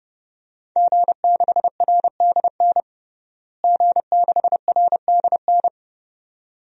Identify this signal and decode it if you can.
Morse code, heard as a keyed tone, about 31 wpm, 700 Hz. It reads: G6RDN G6RDN
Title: DXCC-100 Call Signs - 5 Characters - Repeated 2x - 30wpm